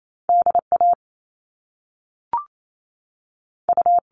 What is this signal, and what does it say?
Morse code, heard as a keyed tone, about 28 wpm, 700 Hz. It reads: DA U
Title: All Cut Numbers - 2-Digit 28wpm